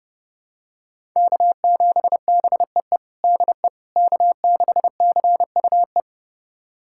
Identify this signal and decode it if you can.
Morse code, heard as a keyed tone, about 30 wpm, 700 Hz. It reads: K7BEE DE K6CUE
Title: USA Callsign DE Callsign - 5 Characters - 30wpm